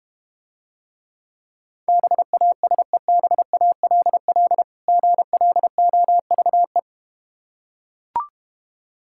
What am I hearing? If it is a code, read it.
Morse code, heard as a keyed tone, about 32 wpm, 700 Hz. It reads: BASEBALL GLOVE